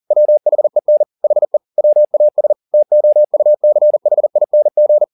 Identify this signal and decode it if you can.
Morse code, heard as a keyed tone, about 40 wpm, 600 Hz. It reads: WHEN HE WAS TOUCHING